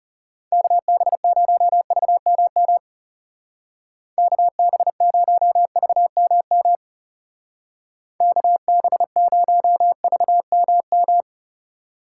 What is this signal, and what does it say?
Morse code, heard as a keyed tone, about 35 wpm, 700 Hz. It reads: KB0VMM KB0VMM KB0VMM
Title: Speed Racing - USA Call Signs 30wpm